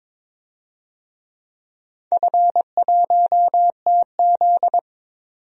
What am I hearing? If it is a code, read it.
Morse code, heard as a keyed tone, about 22 wpm, 700 Hz. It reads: F1TZ
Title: DXCC-100 Call Signs - 4 Characters - 22wpm